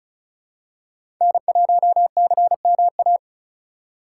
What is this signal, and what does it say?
Morse code, heard as a keyed tone, about 34 wpm, 700 Hz. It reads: N1CMA
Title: USA Call Signs - 5 Characters - 4x Character Spacing - 35wpm